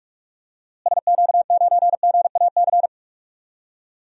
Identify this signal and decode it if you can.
Morse code, heard as a keyed tone, about 46 wpm, 700 Hz. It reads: SQ9GAC
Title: DXCC-100 Call Signs - 6 Characters - Repeated 3x - 45wpm